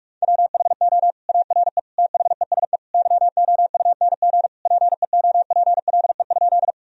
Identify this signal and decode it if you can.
Morse code, heard as a keyed tone, about 41 wpm, 700 Hz. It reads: WHO ARE THESE YOUNG PEOPLE?
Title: Sentences from Top 1000 Words - Review Part 1 - 45wpm